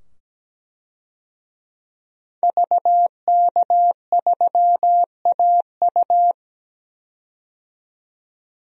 Morse code, heard as a keyed tone, about 17 wpm, 700 Hz. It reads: VK3AU
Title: DXCC-100 Call Signs - 5 Characters - 17wpm